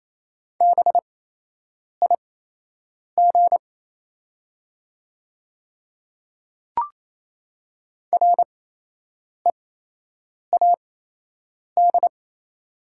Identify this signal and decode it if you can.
Morse code, heard as a keyed tone, about 28 wpm, spaced as for 8 wpm, 700 Hz. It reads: BIG READ
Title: 8x Character Spacing - Top 200 Words - 28wpm